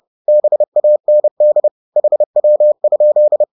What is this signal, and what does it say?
Morse code, heard as a keyed tone, about 30 wpm, 600 Hz. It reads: BAND HW?
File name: 30wpm-7